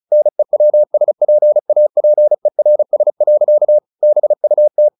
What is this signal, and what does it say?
Morse code, heard as a keyed tone, about 34 wpm, 600 Hz. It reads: NEWSPAPERS. BUT
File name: the old man and the sea 2 35wpm